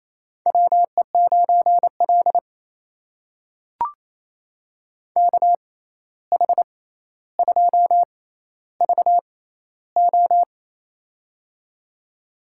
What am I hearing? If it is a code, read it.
Morse code, heard as a keyed tone, about 28 wpm, spaced as for 13 wpm, 700 Hz. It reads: WE9L KH2VO
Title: USA Call Signs - 6x Character Spacing - 28wpm